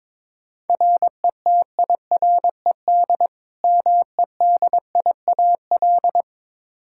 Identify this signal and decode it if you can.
Morse code, heard as a keyed tone, about 22 wpm, 700 Hz. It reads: RETIRED MEDIAL
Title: Practice QSO - Exchange 3 - More Info - 22wpm